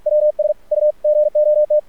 Morse code, heard as a keyed tone, about 44 wpm, 600 Hz. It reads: WS U GOT
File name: archived / CRSnet